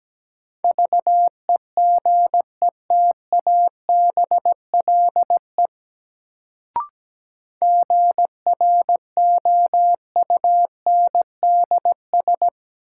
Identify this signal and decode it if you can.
Morse code, heard as a keyed tone, about 17 wpm, 700 Hz. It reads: VEGETABLE GROUNDS
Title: Sentences from Top 1300 Words - New Word Drill Alt - 17wpm